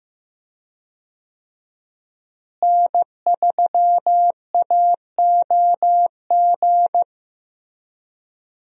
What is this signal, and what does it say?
Morse code, heard as a keyed tone, about 15 wpm, 700 Hz. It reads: N3AOG
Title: USA Call Signs - 5 Characters - Repeated 2x - 15wpm